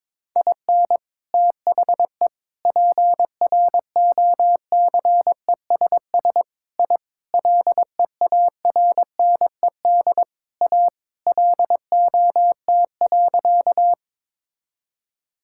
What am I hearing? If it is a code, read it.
Morse code, heard as a keyed tone, about 22 wpm, 700 Hz. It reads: IN THE PROCESS I LEARNED A LOT.
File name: practice / 22wpm / Sentences with 3-Word Phrases - 22wpm